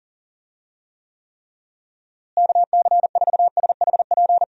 Morse code, heard as a keyed tone, about 40 wpm, 700 Hz. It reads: KC4SHP
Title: USA Call Signs - 6 Characters - Repeated 3x - 40wpm